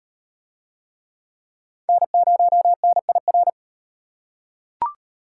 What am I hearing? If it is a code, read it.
Morse code, heard as a keyed tone, about 36 wpm, 700 Hz. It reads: N0NIR